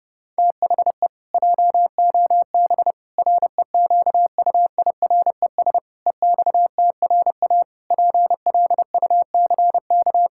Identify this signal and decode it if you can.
Morse code, heard as a keyed tone, about 30 wpm, 700 Hz. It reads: THE JOB REQUIRES EXTRA PLUCK
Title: Pangram Challenge v2 - 30wpm